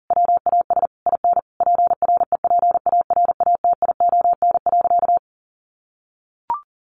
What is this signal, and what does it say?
Morse code, heard as a keyed tone, about 40 wpm, 700 Hz. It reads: WAS IN PREPARATION.
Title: Sentences from Top 1400 Words 40wpm